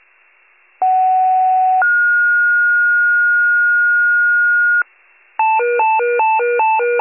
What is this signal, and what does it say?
Two-tone page, heard as a one-second tone, then a three-second tone, about 750 then 1430 Hz. It is heard from 0.8 to 4.8 s.